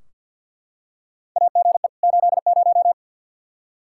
Morse code, heard as a keyed tone, about 51 wpm, 700 Hz. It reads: AGE 80
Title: Practice QSO - Exchange 3 - More Info - 50wpm